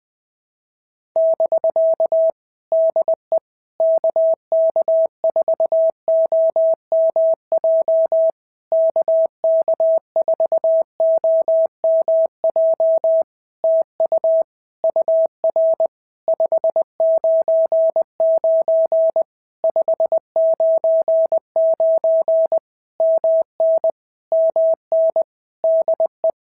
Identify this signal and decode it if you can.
Morse code, heard as a keyed tone, about 20 wpm, 650 Hz. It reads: <BK> DE KK4OMJ KK4OMJ TU UR 599 599 MN MN DE
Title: POTA QSO Practice - 20wpm